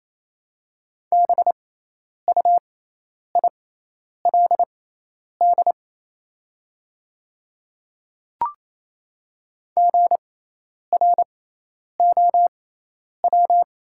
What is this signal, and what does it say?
Morse code, heard as a keyed tone, about 28 wpm, spaced as for 10 wpm, 700 Hz. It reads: BUILD GROW
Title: 6x Character Spacing - Top 300 Words - 28wpm